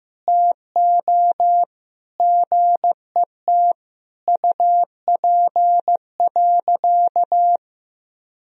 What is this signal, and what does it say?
Morse code, heard as a keyed tone, about 15 wpm, 700 Hz. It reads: TO GET UP.
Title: Sentences from Top 500 Words and Abbreviations 15wpm